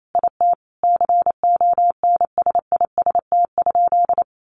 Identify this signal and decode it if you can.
Morse code, heard as a keyed tone, about 28 wpm, 700 Hz. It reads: IT CONSIST?
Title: Sentences from Top 1400 Words 28wpm